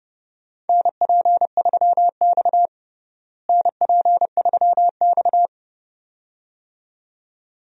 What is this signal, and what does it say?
Morse code, heard as a keyed tone, about 31 wpm, 700 Hz. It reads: NP3X NP3X
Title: USA Call Signs - 4 Characters - Repeated 2x - 30wpm